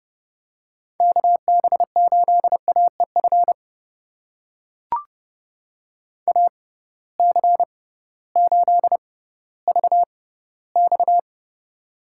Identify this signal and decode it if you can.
Morse code, heard as a keyed tone, about 30 wpm, spaced as for 16 wpm, 700 Hz. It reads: KB8AEF AC8VX
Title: USA Call Signs - 6x Character Spacing - 30wpm